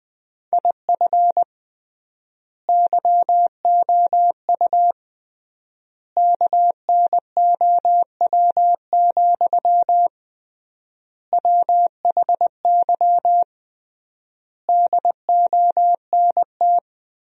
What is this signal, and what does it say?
Morse code, heard as a keyed tone, about 20 wpm, 700 Hz. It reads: IF YOU KNOW, WHY DONT
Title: 3x Word-Spacing - Sentences from Top 300 Words - 20wpm